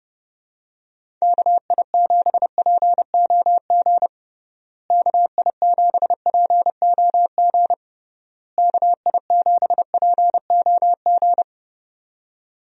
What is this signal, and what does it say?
Morse code, heard as a keyed tone, about 31 wpm, 700 Hz. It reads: KI7POG KI7POG KI7POG
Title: USA Call Signs - 6 Characters - Repeated 3x - 30wpm